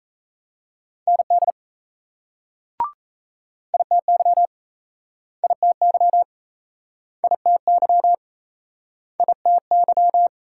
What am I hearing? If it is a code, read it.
Morse code, heard as a keyed tone, about 35 wpm, 700 Hz. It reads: ND ITY ITY ITY ITY
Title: Speed-Racing Multi-letter Phonemes - 28wpm